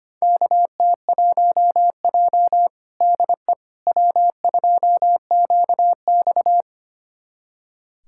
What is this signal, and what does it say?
Morse code, heard as a keyed tone, about 25 wpm, 700 Hz. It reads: KT1J DE W2QX
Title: USA Callsign DE Callsign - 4 Characters - 25wpm